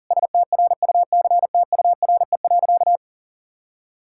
Morse code, heard as a keyed tone, about 40 wpm, 700 Hz. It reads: STRUCTURE.
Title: Sentences from Top 1000 Words - Review Part 6 - 40wpm